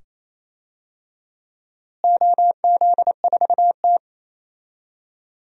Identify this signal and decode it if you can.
Morse code, heard as a keyed tone, about 28 wpm, 700 Hz. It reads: OZ4T
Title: DXCC-100 Call Signs - 4 Characters - Repeated 2x - 28wpm